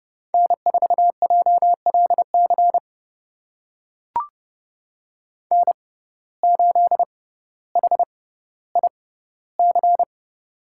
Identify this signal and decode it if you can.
Morse code, heard as a keyed tone, about 30 wpm, spaced as for 15 wpm, 700 Hz. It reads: N4JLC N8HIC
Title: USA Call Signs - 6x Character Spacing - 30wpm